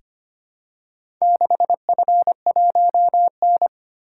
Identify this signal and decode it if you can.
Morse code, heard as a keyed tone, about 25 wpm, 700 Hz. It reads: 6F1N
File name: practice / 25wpm / Four Letter-Number Combinations - 25wpm